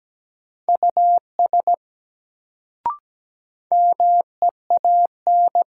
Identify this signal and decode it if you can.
Morse code, heard as a keyed tone, about 17 wpm, 700 Hz. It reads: US MEAN
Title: Sentences from Top 200 Words - New Word Drill - 17wpm